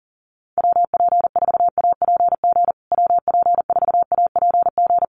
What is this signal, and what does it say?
Morse code, heard as a keyed tone, about 40 wpm, 700 Hz. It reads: WP4APG WP4APG
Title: USA Call Signs Repeated 3x 40wpm